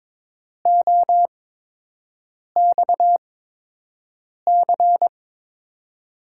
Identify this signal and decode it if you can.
Morse code, heard as a keyed tone, about 22 wpm, spaced as for 6 wpm, 700 Hz. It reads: OXC
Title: USA Call Signs - 6 Characters - 8x Character Spacing - 22wpm